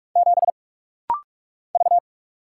Morse code, heard as a keyed tone, about 46 wpm, 700 Hz. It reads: 7 V
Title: Single Letter-Number - Rapid-Fire 45wpm